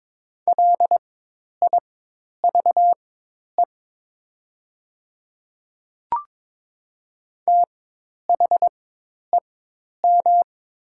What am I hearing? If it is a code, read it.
Morse code, heard as a keyed tone, about 22 wpm, spaced as for 10 wpm, 700 Hz. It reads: LIVE THEM
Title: 4x Character Spacing - Top 200 Words - 22wpm